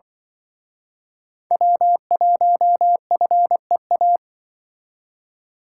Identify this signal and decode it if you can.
Morse code, heard as a keyed tone, about 24 wpm, 700 Hz. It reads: W1FEA
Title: Speed Racing - USA 1x3 Call Signs 20wpm